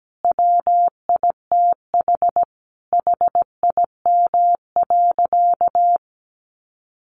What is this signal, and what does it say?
Morse code, heard as a keyed tone, about 17 wpm, 700 Hz. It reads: WITH HIM.
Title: Sentences from Top 1100 Words 17wpm